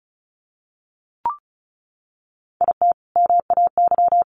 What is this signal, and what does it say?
Morse code, heard as a keyed tone, about 34 wpm, 700 Hz. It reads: IT MAY
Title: Sentences from Top 1000 Words 35wpm